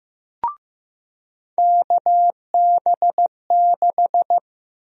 Morse code, heard as a keyed tone, about 15 wpm, 700 Hz. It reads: KB6
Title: USA Call Sign Prefixes - 15wpm